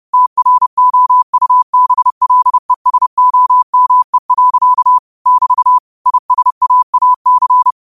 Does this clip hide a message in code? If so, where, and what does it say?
Yes: Morse code from 0.1 to 7.7 s, heard as a keyed tone, about 30 wpm, 1000 Hz. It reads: TROUBLESOME. <BT> ISAAC